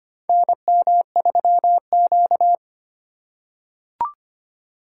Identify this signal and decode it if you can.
Morse code, heard as a keyed tone, about 25 wpm, 700 Hz. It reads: NM3Q
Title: USA Call Signs - 2x Character Spacing - 25wpm